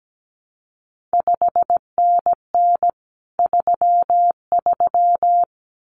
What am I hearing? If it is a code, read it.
Morse code, heard as a keyed tone, about 17 wpm, 700 Hz. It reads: 5NN 33
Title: CQ WW DX Contest Exchange Encore - 17wpm